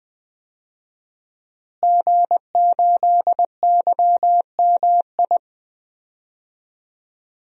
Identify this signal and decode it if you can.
Morse code, heard as a keyed tone, about 20 wpm, 700 Hz. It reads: G8YMI
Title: DXCC-100 Call Signs - 5 Characters - Repeated 3x - 20wpm